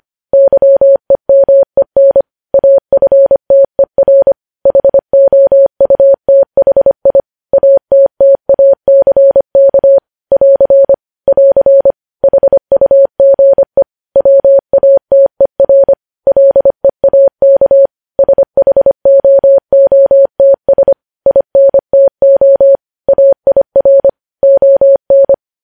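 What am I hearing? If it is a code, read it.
Morse code, heard as a keyed tone, about 25 wpm, 550 Hz. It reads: YEMEN AFTER HOUTHI ATTACK <AR> <AR> HUGE WATER LEAK SHOOTS INTO AIR ON
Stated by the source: News headlines sent in Morse code at 25.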